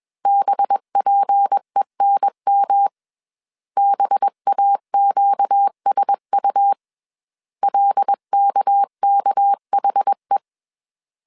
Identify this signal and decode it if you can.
Morse code, heard as a keyed tone, about 21 wpm, 800 Hz. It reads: BPENM BAQSU LKKHE
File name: CW / DET / 1 6 1500C S A 20wpm 2048